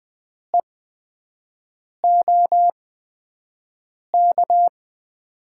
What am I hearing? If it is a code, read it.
Morse code, heard as a keyed tone, about 20 wpm, spaced as for 5 wpm, 700 Hz. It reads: EOK
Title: USA Call Signs - 6 Characters - 8x Character Spacing - 20wpm